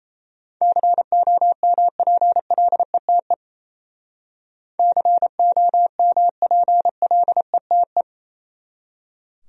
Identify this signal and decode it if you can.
Morse code, heard as a keyed tone, about 34 wpm, 700 Hz. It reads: COMPLETE COMPLETE
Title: Speed-Racing Top 500 Words 28wpm